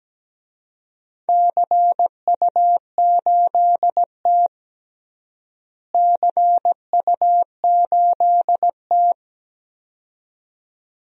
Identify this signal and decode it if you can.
Morse code, heard as a keyed tone, about 17 wpm, 700 Hz. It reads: CU8T CU8T
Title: DXCC-100 Call Signs - 4 Characters - Repeated 2x - 17wpm